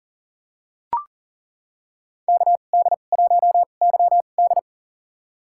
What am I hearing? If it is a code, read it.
Morse code, heard as a keyed tone, about 40 wpm, spaced as for 29 wpm, 700 Hz. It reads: KD1YD
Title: USA Call Signs - 5 Characters - 2x Character Spacing - 40wpm